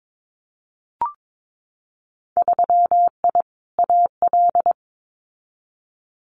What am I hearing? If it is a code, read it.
Morse code, heard as a keyed tone, about 22 wpm, 700 Hz. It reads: 3I AL
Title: Winter Field Day Exchanges Encore 22wpm